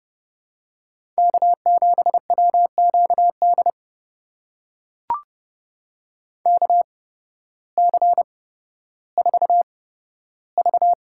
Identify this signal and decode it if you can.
Morse code, heard as a keyed tone, about 30 wpm, spaced as for 14 wpm, 700 Hz. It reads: K7WQD KC4V
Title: USA Call Signs - 5 Characters - 8x Character Spacing - 30wpm